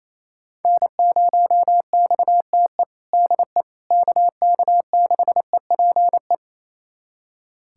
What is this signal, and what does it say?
Morse code, heard as a keyed tone, about 28 wpm, 700 Hz. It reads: N0XTE DE KK6EPE
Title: USA Callsign DE Callsign 28wpm